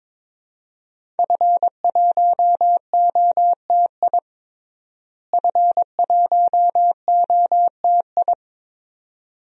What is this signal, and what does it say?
Morse code, heard as a keyed tone, about 22 wpm, 700 Hz. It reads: F1OTI F1OTI
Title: DXCC-100 Call Signs - 5 Characters - Repeated 2x - 22wpm